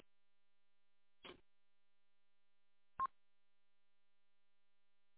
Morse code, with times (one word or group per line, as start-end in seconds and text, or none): none